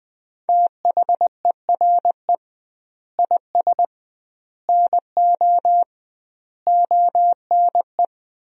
Morse code, heard as a keyed tone, about 20 wpm, 700 Hz. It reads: THERE IS NO ONE
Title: 2x Word-Spacing - Sentences from Top 500 Words - 20wpm